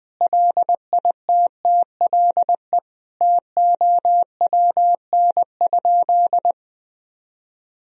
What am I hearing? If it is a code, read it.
Morse code, heard as a keyed tone, about 21 wpm, 700 Hz. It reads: LITTLE TOWN?
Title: Sentences from Top 1000 Words - Review Part 2 - 20wpm